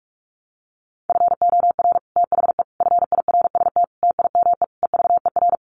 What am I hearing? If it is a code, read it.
Morse code, heard as a keyed tone, about 44 wpm, 700 Hz. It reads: FOR THE FIRST TIME EVER
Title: Sets of 5 Words Encore2 45wpm